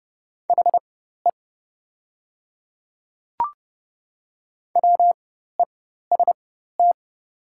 Morse code, heard as a keyed tone, about 30 wpm, spaced as for 14 wpm, 690 Hz. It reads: HE WEST
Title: 4x Character Spacing - Top 500 Words - 30wpm